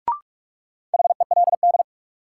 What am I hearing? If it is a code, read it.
Morse code, heard as a keyed tone, about 46 wpm, 700 Hz. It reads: HELD
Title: Sentences from Top 1600 Words - New Word Drill - 45wpm